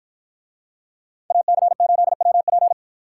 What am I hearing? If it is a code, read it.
Morse code, heard as a keyed tone, about 53 wpm, 700 Hz. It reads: AC8WP